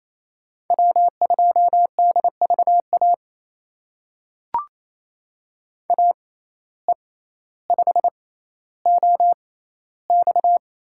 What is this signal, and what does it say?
Morse code, heard as a keyed tone, about 28 wpm, spaced as for 14 wpm, 700 Hz. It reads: W2DVA AE5OX
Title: USA Call Signs - 5 Characters - 6x Character Spacing - 28wpm